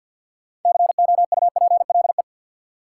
Morse code, heard as a keyed tone, about 47 wpm, 700 Hz. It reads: COUPLE
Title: Sets of 2 Words - New Word Drill - 50wpm